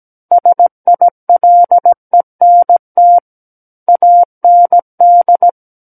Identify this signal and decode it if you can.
Morse code, heard as a keyed tone, about 17 wpm, 700 Hz. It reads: SILENT AND
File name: Book One / 08-Chapter 8 - 17 WPM